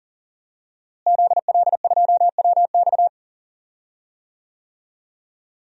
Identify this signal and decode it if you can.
Morse code, heard as a keyed tone, about 37 wpm, 700 Hz. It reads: ZL2WX
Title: DXCC-100 Call Signs - 5 Characters - Repeated 2x - 40wpm